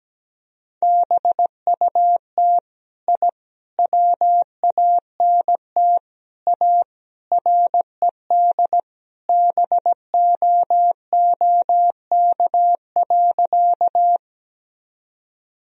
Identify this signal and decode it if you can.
Morse code, heard as a keyed tone, about 17 wpm, 700 Hz. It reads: BUT I WANT A RED BOOK.
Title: Sentences from Top 1000 Words - Review Part 2 - 17wpm